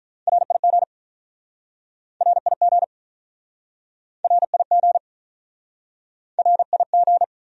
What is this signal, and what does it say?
Morse code, heard as a keyed tone, about 48 wpm, 700 Hz. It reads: RIG RIG RIG RIG